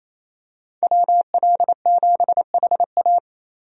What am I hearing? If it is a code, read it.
Morse code, heard as a keyed tone, about 28 wpm, 700 Hz. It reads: WL7HA
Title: USA Call Signs - 5 Characters - Repeated 2x - 28wpm